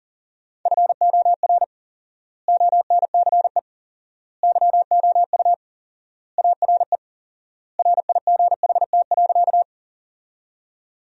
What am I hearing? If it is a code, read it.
Morse code, heard as a keyed tone, about 40 wpm, 700 Hz. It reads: FOR ONCE YOU ARE RIGHT.
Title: 4x Word-Spacing - Sentences from Top 300 Words - 40wpm